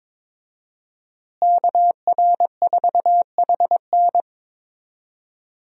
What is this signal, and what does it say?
Morse code, heard as a keyed tone, about 22 wpm, 700 Hz. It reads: KR4HN
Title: USA Call Signs - 5 Characters - Repeated 3x - 22wpm